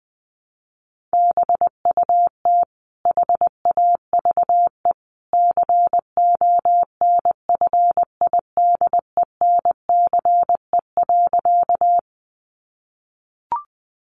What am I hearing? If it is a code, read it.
Morse code, heard as a keyed tone, about 20 wpm, 700 Hz. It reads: BUT HAVE CONFIDENCE.
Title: Sentences from Top 1500 Words 20wpm